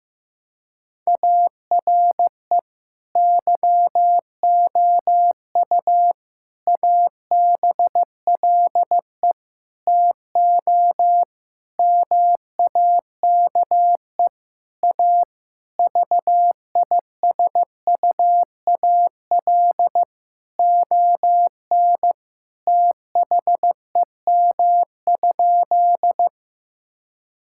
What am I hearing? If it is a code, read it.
Morse code, heard as a keyed tone, about 15 wpm, 700 Hz. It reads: ARE YOU ABLE TO MAKE A VISUAL ON THEM?
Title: Sentences from Top 1000 Words - Review Part 6 - 15wpm